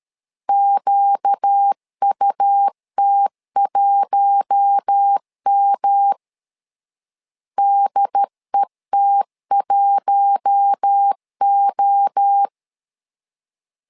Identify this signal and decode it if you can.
Morse code, heard as a keyed tone, about 13 wpm, 800 Hz. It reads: QUT1M DET1O
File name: LEZIONI / 4 Q1S 12wpm 1802